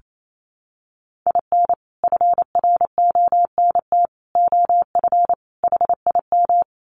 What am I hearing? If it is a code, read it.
Morse code, heard as a keyed tone, about 28 wpm, 700 Hz. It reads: IN FRONT OF HIM
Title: Sets of 4 Words 28wpm